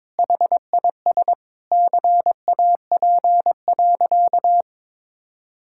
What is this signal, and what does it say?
Morse code, heard as a keyed tone, about 22 wpm, 700 Hz. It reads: HIS CAP.
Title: Sentences from Top 1600 Words 22wpm